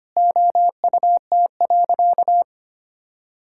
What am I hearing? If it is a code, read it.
Morse code, heard as a keyed tone, about 25 wpm, 700 Hz. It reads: OUT.
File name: practice / 25wpm / Sentences with 9-Word Phrases - 25wpm